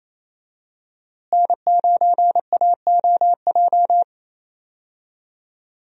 Morse code, heard as a keyed tone, about 28 wpm, 700 Hz. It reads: N9AOJ
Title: USA Call Signs - 5 Characters - Repeated 3x - 28wpm